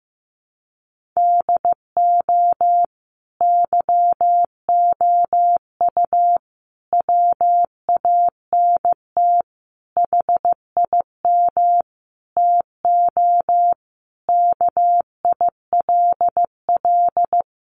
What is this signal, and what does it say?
Morse code, heard as a keyed tone, about 14 wpm, 700 Hz. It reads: DO YOU WANT HIM TO KILL
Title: Sentences from Top 1400 Words 15wpm